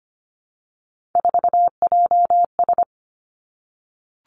Morse code, heard as a keyed tone, about 25 wpm, 700 Hz. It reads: 4JS
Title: Three Challenging Letter-Number Combinations 25wpm